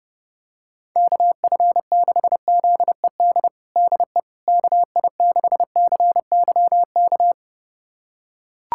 Morse code, heard as a keyed tone, about 30 wpm, 700 Hz. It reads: KF6ZED DE KI6CYK
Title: USA Callsign DE Callsign - 6 Characters - 30wpm